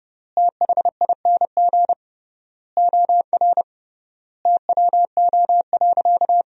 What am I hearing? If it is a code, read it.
Morse code, heard as a keyed tone, about 30 wpm, 700 Hz. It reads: THING OR TWO.
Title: 3x Word-Spacing - Sentences from Top 300 Words - 30wpm